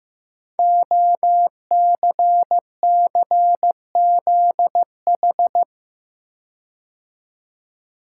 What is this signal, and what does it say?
Morse code, heard as a keyed tone, about 15 wpm, 700 Hz. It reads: OCCZH
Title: Code Groups - 5 Letters - 15wpm